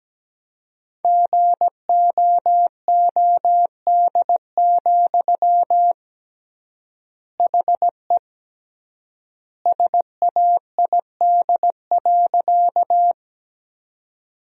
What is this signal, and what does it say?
Morse code, heard as a keyed tone, about 17 wpm, 700 Hz. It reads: GOOD, HE SAID.
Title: 3x Word-Spacing - Sentences from Top 200 Words - 17wpm